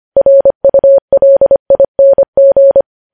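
Morse code, heard as a keyed tone, about 25 wpm, 550 Hz. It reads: RULING
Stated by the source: Morse code news headlines sent at 25 wpm.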